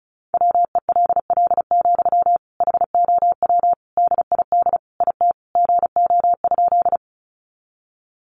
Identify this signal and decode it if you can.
Morse code, heard as a keyed tone, about 34 wpm, 700 Hz. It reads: WELL, HOW DID IT GO?
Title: Sentences from Top 200 Words 35wpm